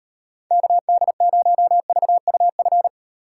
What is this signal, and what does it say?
Morse code, heard as a keyed tone, about 30 wpm, 700 Hz. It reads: KD0VUF
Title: Speed Racing - USA 2x3 Call Signs 25wpm